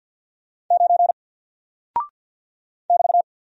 Morse code, heard as a keyed tone, about 50 wpm, 700 Hz. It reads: 9 <BT>
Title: Single Character (Letter:Number:Punctuation:Prosign) Alt 50wpm